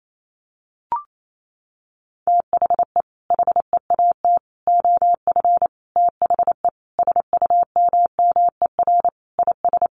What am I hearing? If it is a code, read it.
Morse code, heard as a keyed tone, about 28 wpm, 700 Hz. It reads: THE HEAT OF THE SUMMER IS
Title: Sentences from Top 1100 Words 28wpm